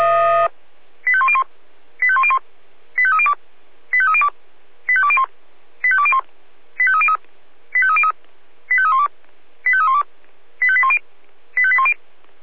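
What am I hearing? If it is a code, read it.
Five-tone selective call (ZVEI), heard as quick runs of five tones, tones of about 70 ms. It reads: 85101 85101 85302 85302 85201 85201 85303 85303 85312 85312 85810 85810
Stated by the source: ZVEI five-tone selective call 85310 received by SDR.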